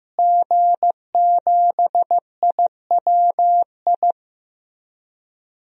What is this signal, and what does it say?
Morse code, heard as a keyed tone, about 15 wpm, 700 Hz. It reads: G7IWI
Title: DXCC-100 Call Signs - 5 Characters - Repeated 3x - 15wpm